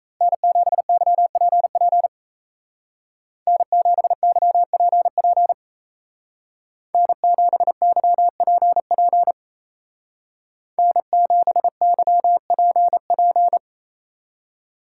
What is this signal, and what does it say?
Morse code, heard as a keyed tone, about 33 wpm, 700 Hz. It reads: N7YPP N7YPP N7YPP N7YPP
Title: Speed Racing - USA 1x3 Call Signs 28wpm